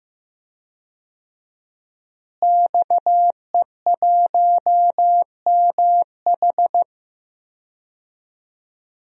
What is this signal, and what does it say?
Morse code, heard as a keyed tone, about 15 wpm, 700 Hz. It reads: XE1MH
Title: DXCC-100 Call Signs - 5 Characters - Repeated 3x - 15wpm